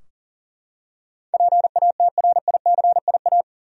Morse code, heard as a keyed tone, about 40 wpm, 700 Hz. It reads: PATRICIA
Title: USA Common Names - 40wpm